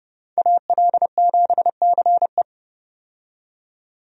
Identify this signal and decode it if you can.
Morse code, heard as a keyed tone, about 30 wpm, 700 Hz. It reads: AL7CE
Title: USA Call Signs - 5 Characters - 30wpm